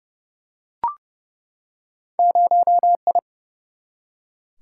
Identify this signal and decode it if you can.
Morse code, heard as a keyed tone, about 30 wpm, 700 Hz. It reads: 0I
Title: Single-Letter Single-Number 30wpm